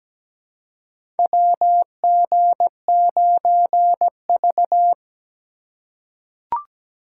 Morse code, heard as a keyed tone, about 17 wpm, 700 Hz. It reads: WG9V
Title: USA Call Signs - 4 Characters - 4x Character Spacing - 17wpm